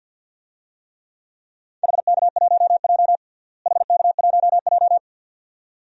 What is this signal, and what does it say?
Morse code, heard as a keyed tone, about 52 wpm, 700 Hz. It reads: HK1J HK1J
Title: DXCC-100 Call Signs - 4 Characters - Repeated 2x - 50wpm